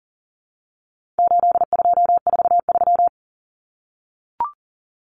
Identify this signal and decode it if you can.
Morse code, heard as a keyed tone, about 40 wpm, 700 Hz. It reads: 8243
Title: Common Cut Numbers - 4-Digit 40wpm